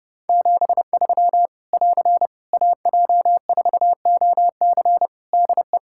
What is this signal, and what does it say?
Morse code, heard as a keyed tone, about 30 wpm, 700 Hz. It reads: 73 <AR> AJ4OC DE
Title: Practice QSO - Exchange 4 - Ending - 30wpm